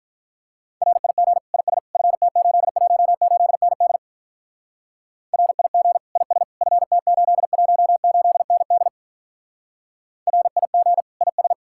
Morse code, heard as a keyed tone, about 44 wpm, 700 Hz. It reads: RIG IS FT818ND RIG IS FT818ND RIG IS